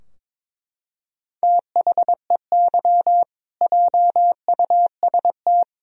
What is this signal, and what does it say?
Morse code, heard as a keyed tone, about 21 wpm, 700 Hz. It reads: THEY JUST
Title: Sentences from Top 1000 Words - Review Part 3 - 22wpm